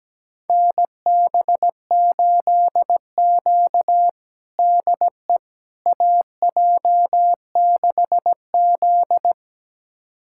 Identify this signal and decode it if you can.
Morse code, heard as a keyed tone, about 17 wpm, 700 Hz. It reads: NB8Q DE AJ6Z
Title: USA Callsign DE Callsign - 4 Characters - 17wpm